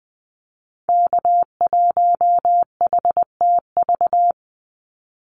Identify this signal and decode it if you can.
Morse code, heard as a keyed tone, about 20 wpm, 700 Hz. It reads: K1HTV
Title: Call Signs - 20wpm